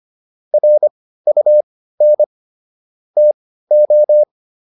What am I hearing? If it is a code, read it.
Morse code, heard as a keyed tone, about 25 wpm, spaced as for 15 wpm, 600 Hz. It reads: RUN TO